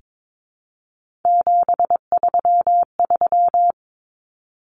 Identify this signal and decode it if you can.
Morse code, heard as a keyed tone, about 22 wpm, 700 Hz. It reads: 733
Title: Common Cut Numbers - 3-Digit 22wpm